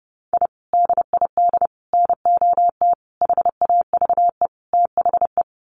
Morse code, heard as a keyed tone, about 30 wpm, 700 Hz. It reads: I DID NOT HAVE THE
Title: Sets of 5 Words Encore 30wpm